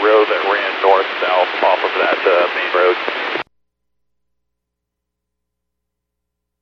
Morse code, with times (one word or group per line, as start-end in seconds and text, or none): none